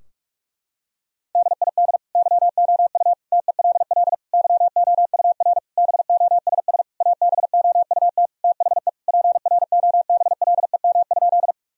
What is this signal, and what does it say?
Morse code, heard as a keyed tone, about 46 wpm, 700 Hz. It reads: DID YOU TELL YOUR BOSS ABOUT THE PROBLEM?
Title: Sentences from Top 1000 Words - Review Part 4 - 45wpm